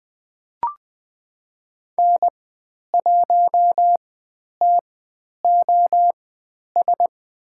Morse code, heard as a keyed tone, about 20 wpm, spaced as for 10 wpm, 700 Hz. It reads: N1TOS
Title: 095 Lesson 16g - Letter c - Call Signs 20wpm